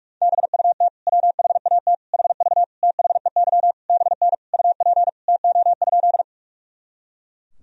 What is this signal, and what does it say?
Morse code, heard as a keyed tone, about 41 wpm, 700 Hz. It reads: BUT WHAT HV THEY BN UP TO?
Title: Sentences from Top 100 Words and Abbreviations 45wpm